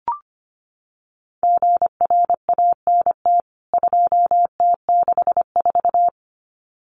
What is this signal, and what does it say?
Morse code, heard as a keyed tone, about 25 wpm, 700 Hz. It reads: GRANT 2T64
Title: CWT Exchange 25wpm